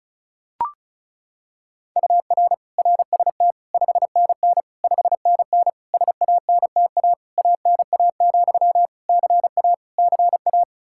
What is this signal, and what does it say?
Morse code, heard as a keyed tone, about 36 wpm, 700 Hz. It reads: UR RST 5NN 5NN SANTA ANA, CA CA
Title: Practice QSO - Exchange 1 - Beginning - 35wpm